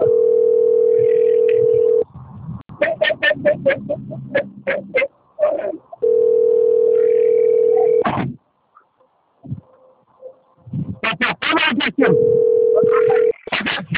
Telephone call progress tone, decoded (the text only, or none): ringback tone